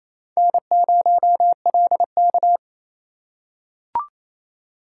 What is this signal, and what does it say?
Morse code, heard as a keyed tone, about 28 wpm, 700 Hz. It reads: N0LK
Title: USA Call Signs - 4 Characters - 2x Character Spacing - 28wpm